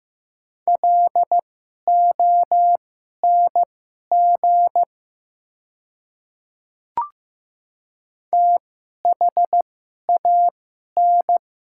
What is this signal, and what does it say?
Morse code, heard as a keyed tone, about 15 wpm, spaced as for 11 wpm, 700 Hz. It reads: LONG THAN
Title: 2x Character Spacing - Top 100 Words - 15wpm